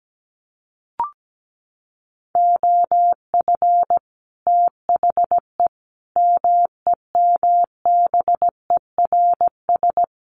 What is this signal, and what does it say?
Morse code, heard as a keyed tone, about 17 wpm, 700 Hz. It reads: OF THE MEMBERS
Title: Sets of 5 Words Encore2 17wpm